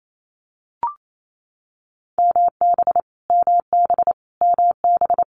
Morse code, heard as a keyed tone, about 28 wpm, 700 Hz. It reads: MB MB MB
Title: Multi-letter Phonemes Repeated 3x 28wpm